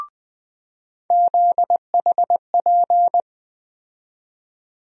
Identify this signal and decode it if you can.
Morse code, heard as a keyed tone, about 21 wpm, 700 Hz. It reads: ZHP
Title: USA Call Sign Postfixes - 3 Letters 20wpm